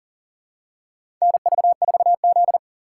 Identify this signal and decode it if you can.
Morse code, heard as a keyed tone, about 40 wpm, 700 Hz. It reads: NV4Z
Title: USA Call Signs - 4 Characters - 6x Character Spacing - 40wpm